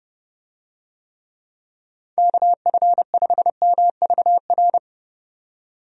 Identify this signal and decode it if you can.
Morse code, heard as a keyed tone, about 30 wpm, 700 Hz. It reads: KF5MVR
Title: USA Call Signs - 6 Characters - 30wpm